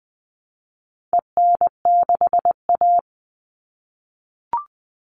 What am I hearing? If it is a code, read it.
Morse code, heard as a keyed tone, about 20 wpm, 700 Hz. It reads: EN6A
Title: All Cut Numbers - 4-Digit 20wpm